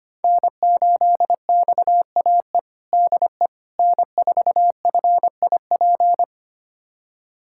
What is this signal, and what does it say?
Morse code, heard as a keyed tone, about 25 wpm, 700 Hz. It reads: N8XAE DE N4FIP
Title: USA Callsign DE Callsign - 5 Characters - 25wpm